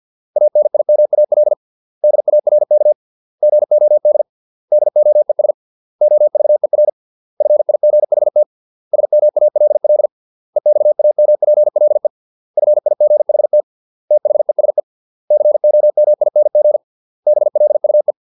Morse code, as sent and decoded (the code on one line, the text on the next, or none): .- -. .. -- .- .-.. / -.. .- .-. -.- / --. --- -.. / -.. --- . ... / --- ...- . .-. / ..-. .. --. .... - / ... -- .- .-.. .-.. / . -..- .- -- .--. .-.. / ..-. .. --. .... - / - .... ... / -.-. --- -- .. -. --. / -... .-.. ..- .
ANIMAL DARK GOD DOES OVER FIGHT SMALL EXAMPLE FIGHT THESE COMING BLUE